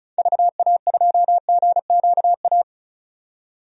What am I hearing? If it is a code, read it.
Morse code, heard as a keyed tone, about 34 wpm, 700 Hz. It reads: VA2GQA
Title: DXCC-100 Call Signs - 6 Characters - 35wpm